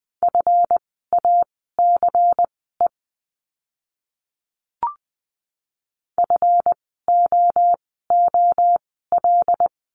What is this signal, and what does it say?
Morse code, heard as a keyed tone, about 20 wpm, spaced as for 15 wpm, 700 Hz. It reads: FACE FOOL
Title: 139 Lesson 23f - Letter f - Words 20wpm